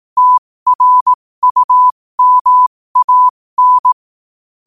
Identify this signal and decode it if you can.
Morse code, heard as a keyed tone, about 18 wpm, spaced as for 15 wpm, 1000 Hz. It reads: TRUMAN
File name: qotdfiles / 15wpm